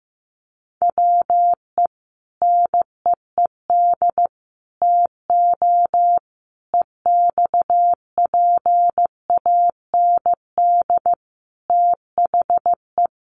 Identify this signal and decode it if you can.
Morse code, heard as a keyed tone, about 15 wpm, 700 Hz. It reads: WE NEED TO EXPAND THE
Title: Sentences from Top 1400 Words 15wpm